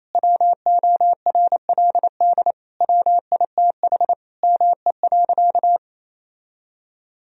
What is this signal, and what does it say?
Morse code, heard as a keyed tone, about 28 wpm, 700 Hz. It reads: WORLD WITH ME.
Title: Sentences from Top 1000 Words - Review Part 3 - 28wpm